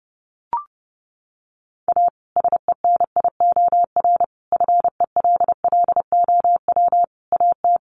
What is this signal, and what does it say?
Morse code, heard as a keyed tone, about 30 wpm, 700 Hz. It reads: A SENIOR FELLOW AT
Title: Sets of 5 Words Encore 30wpm